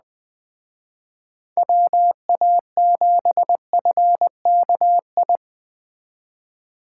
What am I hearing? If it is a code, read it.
Morse code, heard as a keyed tone, about 20 wpm, 700 Hz. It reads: WA7FKI
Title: Speed Racing - USA 2x3 Call Signs 17wpm